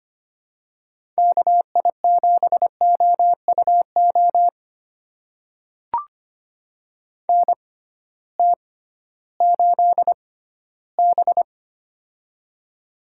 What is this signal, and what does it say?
Morse code, heard as a keyed tone, about 25 wpm, spaced as for 15 wpm, 700 Hz. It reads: KI7OUO NT8B
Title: USA Call Signs - 6x Character Spacing - 25wpm